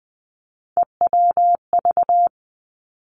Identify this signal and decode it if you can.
Morse code, heard as a keyed tone, about 20 wpm, 700 Hz. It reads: EWV
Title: Three Challenging Letter-Number Combinations 20wpm